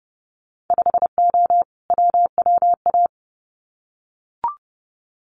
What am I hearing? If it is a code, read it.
Morse code, heard as a keyed tone, about 30 wpm, 700 Hz. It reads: 5O WWA
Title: Winter Field Day Exchanges Encore 30wpm